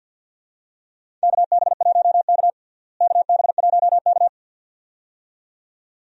Morse code, heard as a keyed tone, about 52 wpm, 700 Hz. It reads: KB1K KB1K
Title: USA Call Signs - 4 Characters - Repeated 2x - 50wpm